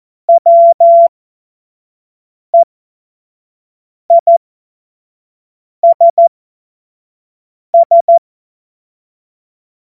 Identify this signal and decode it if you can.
Morse code, heard as a keyed tone, about 14 wpm, spaced as for 5 wpm, 680 Hz. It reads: WEISS
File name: Morsekurs / Witze / Witz 09. November 2021, 25 F70